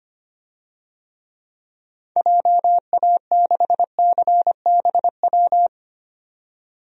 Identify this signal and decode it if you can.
Morse code, heard as a keyed tone, about 25 wpm, 700 Hz. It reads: JA6CBW
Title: DXCC-100 Call Signs - 6 Characters - 25wpm